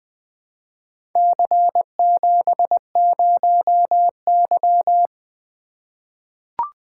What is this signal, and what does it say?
Morse code, heard as a keyed tone, about 20 wpm, 700 Hz. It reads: C70Y
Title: Four Letter-Number Combinations - 20wpm